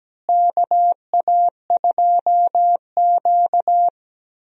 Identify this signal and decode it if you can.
Morse code, heard as a keyed tone, about 17 wpm, 700 Hz. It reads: KA2Q
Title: USA Call Signs - 4 Characters - Repeated 2x - 17wpm